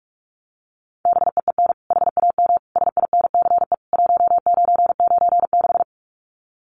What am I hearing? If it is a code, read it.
Morse code, heard as a keyed tone, about 44 wpm, 700 Hz. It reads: BEEN HAM SINCE 1996
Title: Common QSO Phrases 45wpm